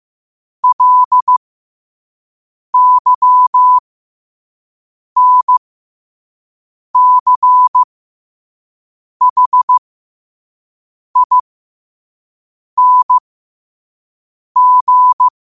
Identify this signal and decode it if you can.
Morse code, heard as a keyed tone, about 15 wpm, spaced as for 5 wpm, 1000 Hz. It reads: LYNCHING
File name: qotdfiles / 5wpm